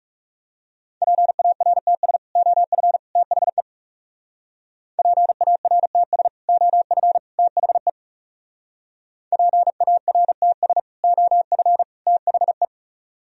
Morse code, heard as a keyed tone, about 39 wpm, 700 Hz. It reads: PARTS OF THE PARTS OF THE PARTS OF THE